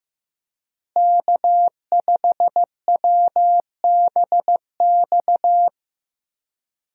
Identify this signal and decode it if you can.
Morse code, heard as a keyed tone, about 15 wpm, 700 Hz. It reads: K5WBX
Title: USA Call Signs - 5 Characters - Repeated 2x - 15wpm